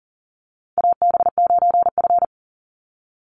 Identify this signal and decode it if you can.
Morse code, heard as a keyed tone, about 40 wpm, 700 Hz. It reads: AB9F
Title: USA Call Signs 40wpm